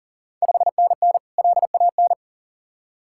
Morse code, heard as a keyed tone, about 39 wpm, 690 Hz. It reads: 5NN LAN
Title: Ontario QSO Party Exchange 40wpm